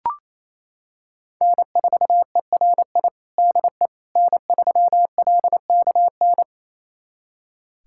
Morse code, heard as a keyed tone, about 27 wpm, 700 Hz. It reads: N4ERI DE N3LKN
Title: USA Callsign DE Callsign - 5 Characters - 28wpm